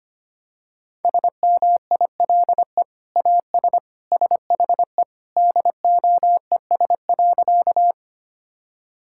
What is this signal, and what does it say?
Morse code, heard as a keyed tone, about 25 wpm, 700 Hz. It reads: SMILE AS SHE DOES.
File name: practice / 25wpm / Sentences with 2-Word Phrases - 25wpm